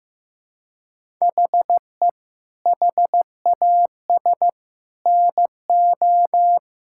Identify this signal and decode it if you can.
Morse code, heard as a keyed tone, about 15 wpm, 700 Hz. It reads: HE HAS NO
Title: Sentences from Top 1000 Words - Review Part 5 - 15wpm